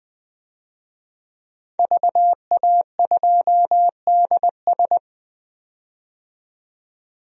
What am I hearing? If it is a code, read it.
Morse code, heard as a keyed tone, about 20 wpm, 700 Hz. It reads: VA2DS
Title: DXCC-100 Call Signs - 5 Characters - 20wpm